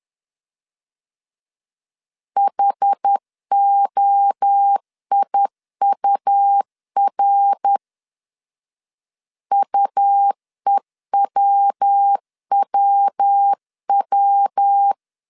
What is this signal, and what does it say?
Morse code, heard as a keyed tone, about 10 wpm, 800 Hz. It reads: HOIUR UEWWW
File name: CW / DET / 1 3 1500C S C 10wpm 3335